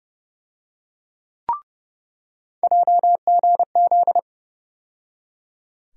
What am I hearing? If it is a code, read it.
Morse code, heard as a keyed tone, about 30 wpm, 700 Hz. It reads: JGZ
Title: USA Call Sign Postfixes - 3 Letters 30wpm